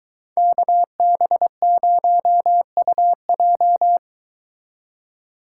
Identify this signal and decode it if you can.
Morse code, heard as a keyed tone, about 23 wpm, 700 Hz. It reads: KB0UJ